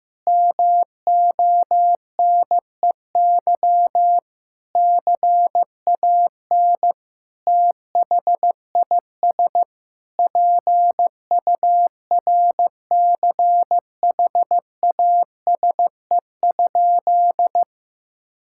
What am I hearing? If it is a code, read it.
Morse code, heard as a keyed tone, about 15 wpm, 700 Hz. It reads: MONEY CAN THIS PURCHASE?
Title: Sentences from Top 1000 Words - Review Part 5 - 15wpm